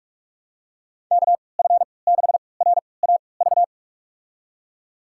Morse code, heard as a keyed tone, about 46 wpm, spaced as for 26 wpm, 700 Hz. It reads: KF6RAV
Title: USA Call Signs - 6 Characters - 3x Character Spacing - 45wpm